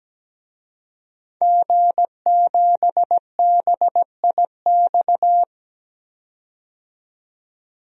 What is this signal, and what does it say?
Morse code, heard as a keyed tone, about 17 wpm, 700 Hz. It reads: G7BIX
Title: DXCC-100 Call Signs - 5 Characters - 17wpm